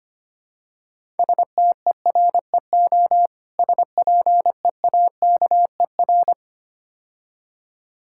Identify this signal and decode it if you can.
Morse code, heard as a keyed tone, about 26 wpm, 700 Hz. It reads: STEREO SPEAKER
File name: practice / 25wpm / Difficult Words - Speed-Racing with Hints - 25wpm